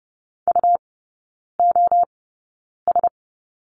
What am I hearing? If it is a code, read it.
Morse code, heard as a keyed tone, about 30 wpm, spaced as for 9 wpm, 690 Hz. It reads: UOS
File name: Lesson 17e - Letter u - Three Characters 30wpm